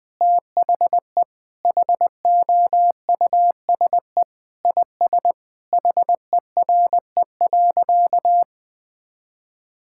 Morse code, heard as a keyed tone, about 20 wpm, 700 Hz. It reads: THE HOUSE IS HERE.
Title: Sentences with 3-Word Phrases - Spoken Beginning - 20wpm